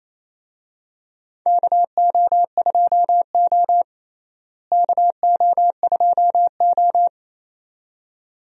Morse code, heard as a keyed tone, about 28 wpm, 700 Hz. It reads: KO2O KO2O
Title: USA Call Signs - 4 Characters - Repeated 2x - 28wpm